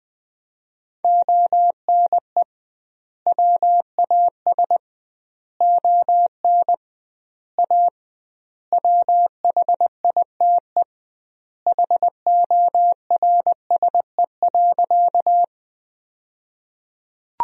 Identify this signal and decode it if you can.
Morse code, heard as a keyed tone, about 20 wpm, 700 Hz. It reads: ONE WAS ON A WHITE HORSE.
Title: 2x Word-Spacing - Sentences from Top 300 Words - 20wpm